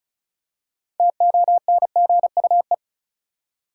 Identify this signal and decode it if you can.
Morse code, heard as a keyed tone, about 36 wpm, 700 Hz. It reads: TONGUE
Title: Six Letter Words - 35wpm